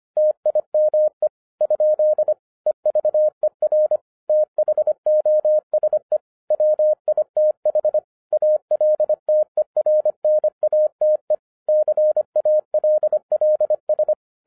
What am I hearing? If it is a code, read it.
Morse code, heard as a keyed tone, about 25 wpm, 610 Hz. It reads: TIME ? EVER THOSE WITH ALTERNATE CALLS
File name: 2012-01-mp3 / n14-FUN WITH NUMBERS-25wpm